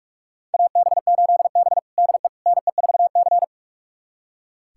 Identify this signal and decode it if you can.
Morse code, heard as a keyed tone, about 44 wpm, 700 Hz. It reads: AB8B DE NE4C
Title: USA Callsign DE Callsign - 4 Characters - 45wpm